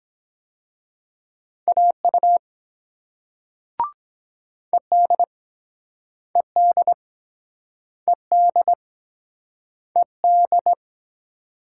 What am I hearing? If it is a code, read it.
Morse code, heard as a keyed tone, about 23 wpm, 700 Hz. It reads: AU ED ED ED ED